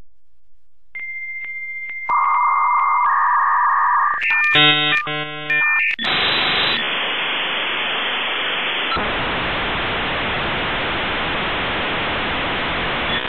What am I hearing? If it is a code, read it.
Dial-up modem handshake.
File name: The 56k dialup modem sound